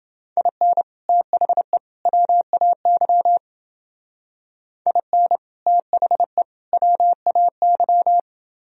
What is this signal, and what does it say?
Morse code, heard as a keyed tone, about 29 wpm, 700 Hz. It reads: IN THE WAY IN THE WAY